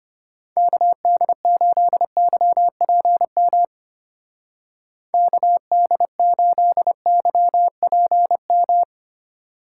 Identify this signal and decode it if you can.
Morse code, heard as a keyed tone, about 32 wpm, 700 Hz. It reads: KD8YPM KD8YPM